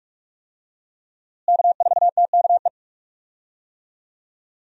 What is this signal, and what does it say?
Morse code, heard as a keyed tone, about 44 wpm, 700 Hz. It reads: K4TKE
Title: USA Call Signs - 5 Characters - Repeated 3x - 45wpm